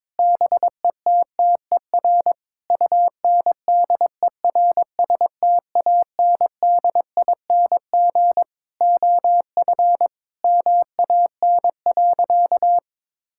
Morse code, heard as a keyed tone, about 22 wpm, 700 Hz. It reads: BETTER UNDERSTANDING OF MAN.
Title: Sentences from Top 1000 Words - Review Part 5 - 22wpm